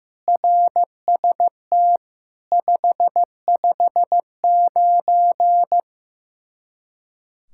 Morse code, heard as a keyed tone, about 15 wpm, 700 Hz. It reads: RST 559
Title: Practice QSO - Exchange 1 - Beginning - 15wpm